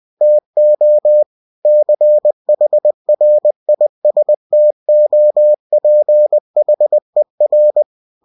Morse code, heard as a keyed tone, about 20 wpm, 600 Hz. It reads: TO CHRISTOPHER
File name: cw-books / winnie-the-pooh / winnie-20wpm-page-0043